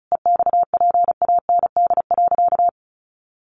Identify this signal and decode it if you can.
Morse code, heard as a keyed tone, about 34 wpm, 700 Hz. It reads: EXPAND.
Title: Sentences from Top 1400 Words 35wpm